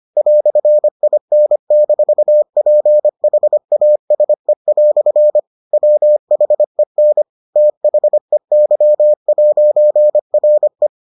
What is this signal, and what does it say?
Morse code, heard as a keyed tone, about 25 wpm, 600 Hz. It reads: "IN-PHASE" WHEN THEY'RE